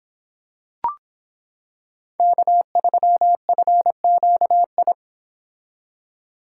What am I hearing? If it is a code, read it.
Morse code, heard as a keyed tone, about 26 wpm, 700 Hz. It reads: K3FQI